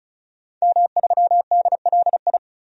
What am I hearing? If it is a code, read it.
Morse code, heard as a keyed tone, about 36 wpm, 700 Hz. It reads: M3DLI
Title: DXCC-100 Call Signs - 5 Characters - 35wpm